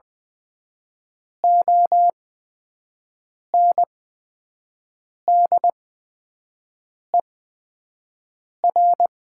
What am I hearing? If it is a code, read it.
Morse code, heard as a keyed tone, about 20 wpm, spaced as for 5 wpm, 700 Hz. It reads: ONDER
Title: 8x Character Spacing - Top 500 Words - 20wpm